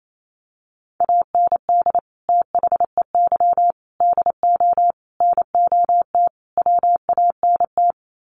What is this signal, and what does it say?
Morse code, heard as a keyed tone, about 28 wpm, 700 Hz. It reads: AND THEY DO NOT WANT
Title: Sets of 5 Words Encore2 28wpm